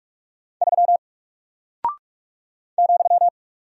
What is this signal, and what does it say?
Morse code, heard as a keyed tone, about 46 wpm, 700 Hz. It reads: 3 ,
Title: Single Character (Letter:Number:Punctuation:Prosign) Alt 45wpm